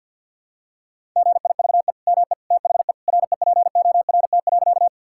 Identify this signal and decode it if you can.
Morse code, heard as a keyed tone, about 50 wpm, 700 Hz. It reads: GIVE ME THE REPORT.
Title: Sentences from Top 1000 Words - Review Part 6 - 50wpm